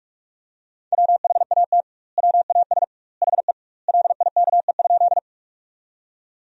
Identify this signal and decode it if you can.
Morse code, heard as a keyed tone, about 46 wpm, 700 Hz. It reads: WHAT WAS HE LIKE?
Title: 2x Word-Spacing - Sentences from Top 100 Words - 45wpm